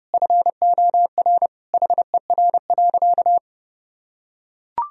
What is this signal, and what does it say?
Morse code, heard as a keyed tone, about 30 wpm, 700 Hz. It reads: FOR HER.
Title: Sentences from Top 1000 Words - Review Part 4 - 30wpm